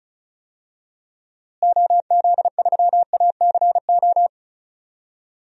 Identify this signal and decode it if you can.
Morse code, heard as a keyed tone, about 36 wpm, 700 Hz. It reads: OZ3ACO
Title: DXCC-100 Call Signs - 6 Characters - Repeated 3x - 35wpm